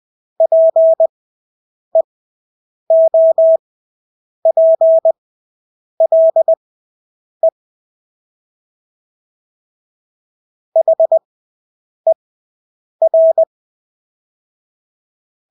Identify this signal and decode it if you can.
Morse code, heard as a keyed tone, about 20 wpm, spaced as for 8 wpm, 650 Hz. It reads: PEOPLE HER
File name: Engelske ord 20/8 wpm